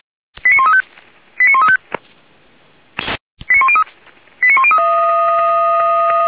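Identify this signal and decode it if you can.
Five-tone selective call (ZVEI), heard as quick runs of five tones, tones of about 70 ms. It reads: 89135 89135 89103 89103
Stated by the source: Five-tone ZVEI selective call received by SDR.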